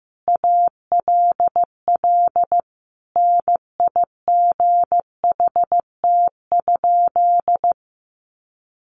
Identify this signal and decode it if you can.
Morse code, heard as a keyed tone, about 14 wpm, 700 Hz. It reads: ALL NIGHT?
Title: Sentences from Top 700 Words 15wpm